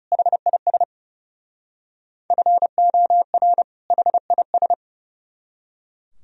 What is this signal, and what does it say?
Morse code, heard as a keyed tone, about 36 wpm, 700 Hz. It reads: HIS FOR HIS
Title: Speed-Racing - Sets of 2 Words 30wpm